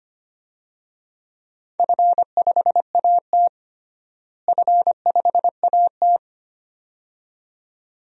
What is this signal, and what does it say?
Morse code, heard as a keyed tone, about 25 wpm, 700 Hz. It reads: F5AT F5AT
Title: DXCC-100 Call Signs - 4 Characters - Repeated 2x - 25wpm